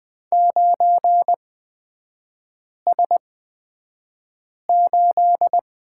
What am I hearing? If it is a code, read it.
Morse code, heard as a keyed tone, about 20 wpm, spaced as for 5 wpm, 700 Hz. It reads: 9S8
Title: 243 Lesson 39e - Letter z - Three Characters 20wpm